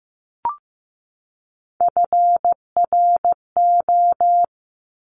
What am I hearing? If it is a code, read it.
Morse code, heard as a keyed tone, about 15 wpm, 700 Hz. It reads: FRO
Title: Top Three Letter Combinations in Words 15wpm